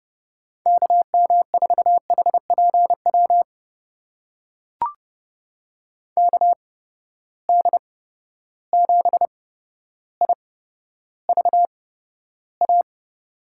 Morse code, heard as a keyed tone, about 31 wpm, spaced as for 13 wpm, 700 Hz. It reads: KM4HPW KD7IVA
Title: USA Call Signs - 6 Characters - 8x Character Spacing - 30wpm